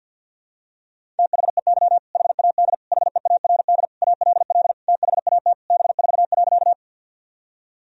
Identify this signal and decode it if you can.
Morse code, heard as a keyed tone, about 51 wpm, 700 Hz. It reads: THEY HAD HEARD ALL THAT B4.
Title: Sentences from Top 400 Words and Abbreviations 50wpm